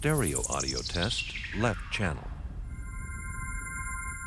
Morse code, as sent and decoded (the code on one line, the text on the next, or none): none